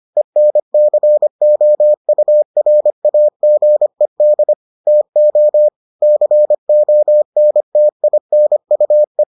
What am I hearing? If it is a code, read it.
Morse code, heard as a keyed tone, about 25 wpm, 600 Hz. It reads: ENCOURAGED TO CONTINUE